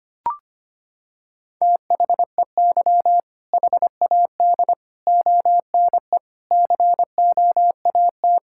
Morse code, heard as a keyed tone, about 25 wpm, 700 Hz. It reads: THEY HAD ONE COAT
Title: Sentences from Top 1600 Words 25wpm